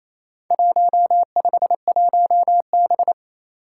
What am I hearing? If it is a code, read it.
Morse code, heard as a keyed tone, about 27 wpm, 700 Hz. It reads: 151B
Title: Four Challenging Letter-Number Combinations 28wpm